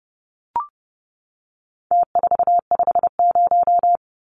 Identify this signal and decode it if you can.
Morse code, heard as a keyed tone, about 30 wpm, 700 Hz. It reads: T450
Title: Numbers - Contest Style - 4-Digit 30wpm